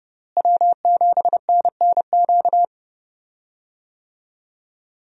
Morse code, heard as a keyed tone, about 29 wpm, 700 Hz. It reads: W7NNQ
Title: USA Call Signs - 5 Characters - Repeated 2x - 30wpm